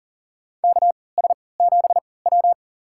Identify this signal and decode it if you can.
Morse code, heard as a keyed tone, about 37 wpm, spaced as for 22 wpm, 700 Hz. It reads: KS7W
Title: USA Call Signs - 4 Characters - 3x Character Spacing - 40wpm